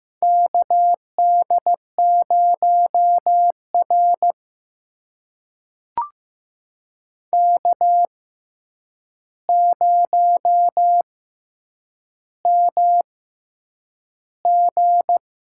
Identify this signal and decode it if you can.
Morse code, heard as a keyed tone, about 15 wpm, spaced as for 8 wpm, 700 Hz. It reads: KD0R K0MG
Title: USA Call Signs - 4 Characters - 6x Character Spacing - 15wpm